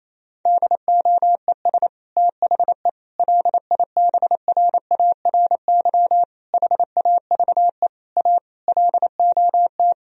Morse code, heard as a keyed tone, about 27 wpm, 700 Hz. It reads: DOES THE LIBRARY HAVE A LOT
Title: Sentences from Top 1000 Words - Review Part 4 - 28wpm